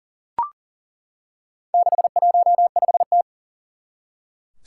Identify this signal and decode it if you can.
Morse code, heard as a keyed tone, about 40 wpm, 700 Hz. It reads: 615T
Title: Four Challenging Letter-Number Combinations 40wpm